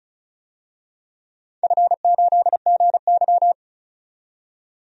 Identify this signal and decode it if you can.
Morse code, heard as a keyed tone, about 34 wpm, 700 Hz. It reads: F8GY
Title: DXCC-100 Call Signs - 4 Characters - 35wpm